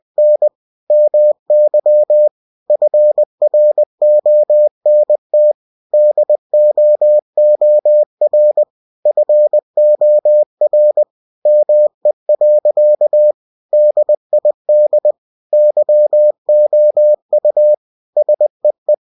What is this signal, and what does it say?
Morse code, heard as a keyed tone, about 20 wpm, 600 Hz. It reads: N MY FRONT DOOR FOR ME. DID YOU SEE
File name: cw-books / winnie-the-pooh / winnie-20wpm-page-0031